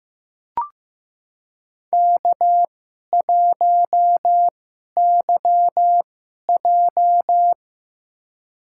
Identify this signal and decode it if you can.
Morse code, heard as a keyed tone, about 15 wpm, spaced as for 11 wpm, 700 Hz. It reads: K1YJ
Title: USA Call Signs - 4 Characters - 2x Character Spacing - 15wpm